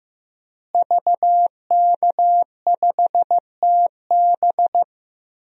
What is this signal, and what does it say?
Morse code, heard as a keyed tone, about 15 wpm, 700 Hz. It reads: VK5TB
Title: DXCC-100 Call Signs - 5 Characters - Repeated 3x - 15wpm